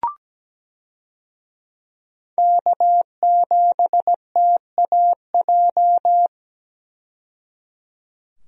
Morse code, heard as a keyed tone, about 17 wpm, 700 Hz. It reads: K7TAJ
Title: USA Call Signs - 5 Characters - 17wpm